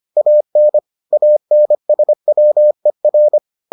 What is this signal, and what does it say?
Morse code, heard as a keyed tone, about 25 wpm, 600 Hz. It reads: AN ANSWER